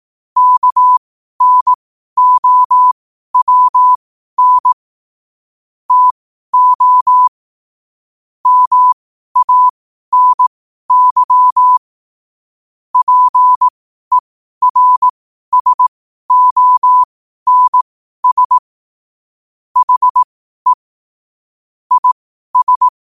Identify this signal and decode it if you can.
Morse code, heard as a keyed tone, about 18 wpm, spaced as for 12 wpm, 1000 Hz. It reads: KNOWN TO MANY PERSONS HE IS
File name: qotdfiles / 12wpm